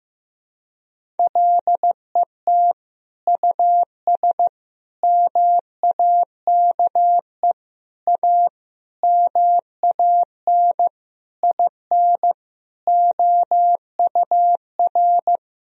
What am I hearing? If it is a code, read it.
Morse code, heard as a keyed tone, about 15 wpm, 700 Hz. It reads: LET US MAKE A MAN IN OUR
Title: Sentences from Top 1000 Words - Review Part 4 - 15wpm